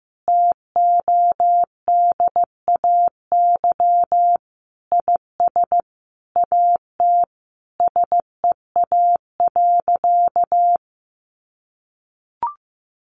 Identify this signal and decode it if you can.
Morse code, heard as a keyed tone, about 15 wpm, 700 Hz. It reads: TODAY IS AT SEA.
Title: Sentences from Top 1400 Words 15wpm